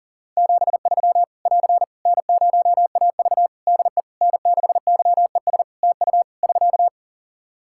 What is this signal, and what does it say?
Morse code, heard as a keyed tone, about 40 wpm, 700 Hz. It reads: 73 <AR> N0AV DE N6YES TU <SK>
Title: Practice QSO - Exchange 4 - Ending - 40wpm